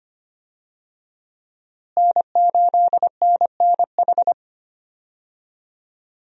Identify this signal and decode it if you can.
Morse code, heard as a keyed tone, about 25 wpm, 700 Hz. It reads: N8NNH
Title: USA Call Signs - 5 Characters - Repeated 2x - 25wpm